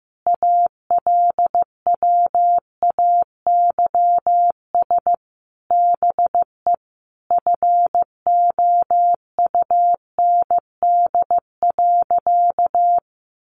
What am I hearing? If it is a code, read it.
Morse code, heard as a keyed tone, about 15 wpm, 700 Hz. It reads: ALWAYS BE FOUND.
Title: Sentences from Top 1300 Words 15wpm